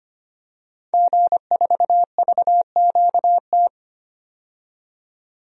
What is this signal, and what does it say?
Morse code, heard as a keyed tone, about 25 wpm, 700 Hz. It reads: G4VQT
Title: DXCC-100 Call Signs - 5 Characters - 25wpm